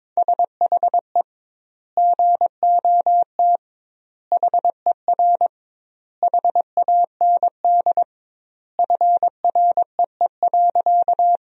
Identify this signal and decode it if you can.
Morse code, heard as a keyed tone, about 22 wpm, 700 Hz. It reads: SHE GOT HER HAND FREE.
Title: 2x Word-Spacing - Sentences from Top 500 Words - 22wpm